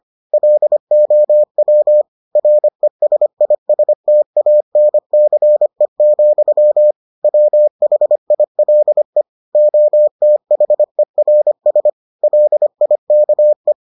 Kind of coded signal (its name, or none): Morse code